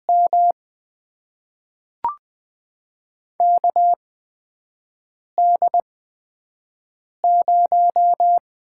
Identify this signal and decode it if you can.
Morse code, heard as a keyed tone, about 20 wpm, spaced as for 5 wpm, 700 Hz. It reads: M KD0
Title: USA Call Signs - 8x Character Spacing - 20wpm